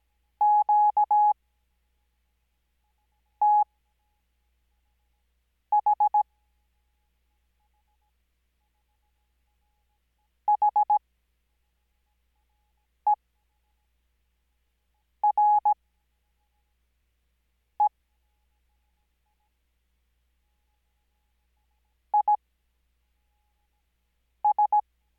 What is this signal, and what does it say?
Morse code, heard as a keyed tone, about 17 wpm, spaced as for 4 wpm, 840 Hz. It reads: QTH HERE IS
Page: Farnsworth Spacing: 15WPM Character speed Spaced at 5WPM.